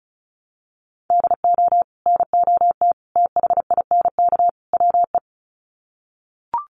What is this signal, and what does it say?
Morse code, heard as a keyed tone, about 34 wpm, 700 Hz. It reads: DO NOT THINK WE
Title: Sets of 4 Words 35wpm